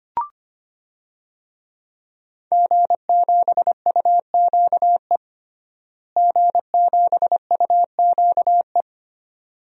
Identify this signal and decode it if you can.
Morse code, heard as a keyed tone, about 25 wpm, 700 Hz. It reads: G7UQE G7UQE
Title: DXCC-100 Call Signs - 5 Characters - Repeated 2x - 25wpm